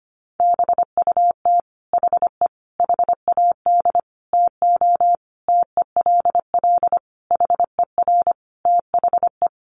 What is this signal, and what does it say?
Morse code, heard as a keyed tone, about 25 wpm, 700 Hz. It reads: BUT HE HAD TO TELL HER THE
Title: Sentences from Top 800 Words 25wpm